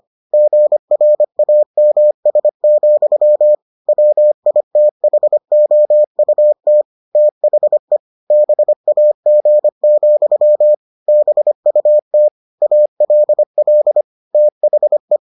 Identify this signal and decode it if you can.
Morse code, heard as a keyed tone, about 25 wpm, 600 Hz. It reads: GRAMS, WITHOUT THE BAG, BUT ALL THE